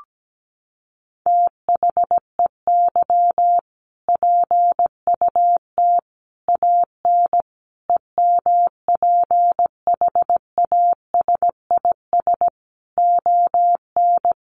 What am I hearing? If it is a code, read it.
Morse code, heard as a keyed tone, about 17 wpm, 700 Hz. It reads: THEY PUT AN EMPHASIS ON
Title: Sentences from Top 1400 Words 17wpm